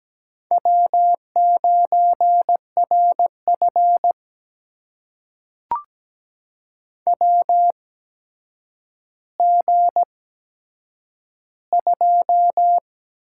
Morse code, heard as a keyed tone, about 17 wpm, spaced as for 8 wpm, 700 Hz. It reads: W9RF WG2
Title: USA Call Signs - 4 Characters - 8x Character Spacing - 17wpm